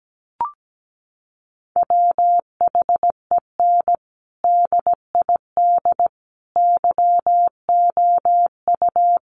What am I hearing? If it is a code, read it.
Morse code, heard as a keyed tone, about 17 wpm, 700 Hz. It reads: WHEN DID YOU
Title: Sentences from Top 300 Words 17wpm